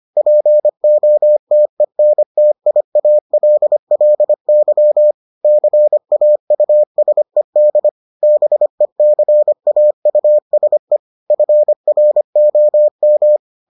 Morse code, heard as a keyed tone, about 25 wpm, 600 Hz. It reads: POTENTIALLY CAUSED BECAUSE FROM